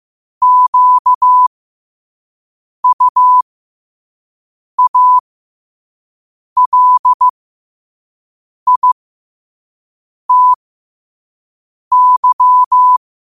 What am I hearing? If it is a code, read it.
Morse code, heard as a keyed tone, about 15 wpm, spaced as for 5 wpm, 1000 Hz. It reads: QUALITY
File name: qotdfiles / 5wpm